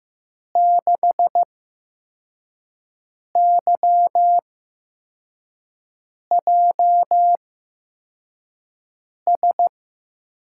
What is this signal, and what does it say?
Morse code, heard as a keyed tone, about 15 wpm, spaced as for 4 wpm, 700 Hz. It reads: 6YJS
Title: USA Call Signs - 6 Characters - 8x Character Spacing - 15wpm